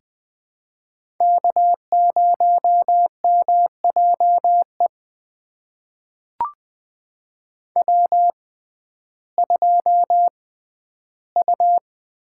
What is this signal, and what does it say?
Morse code, heard as a keyed tone, about 20 wpm, spaced as for 12 wpm, 700 Hz. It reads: K0MJE W2U
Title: USA Call Signs - 5 Characters - 6x Character Spacing - 20wpm